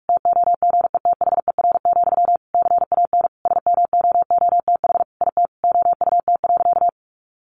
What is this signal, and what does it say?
Morse code, heard as a keyed tone, about 44 wpm, 700 Hz. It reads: TOGETHER, CAN SMOOTH IT OUT.
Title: Sentences from Top 1400 Words 45wpm